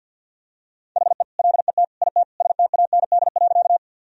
Morse code, heard as a keyed tone, about 48 wpm, 700 Hz. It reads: HE LET IT STAND.
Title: Sentences from Top 1000 Words - Review Part 1 - 50wpm